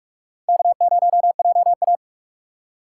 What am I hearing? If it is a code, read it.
Morse code, heard as a keyed tone, about 46 wpm, 700 Hz. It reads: K0JA
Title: USA Call Signs - 4 Characters - 3x Character Spacing - 45wpm